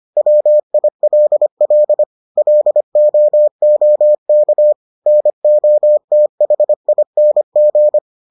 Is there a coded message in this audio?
Yes: Morse code from 0.2 to 8.0 s, heard as a keyed tone, about 25 wpm, 600 Hz. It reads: WILL LOOK NOTHING